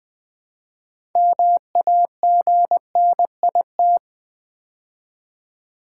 Morse code, heard as a keyed tone, about 20 wpm, 700 Hz. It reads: MAGNIT
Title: Build A Word - 3 Letters At A Time - 9-Letter Words - 20wpm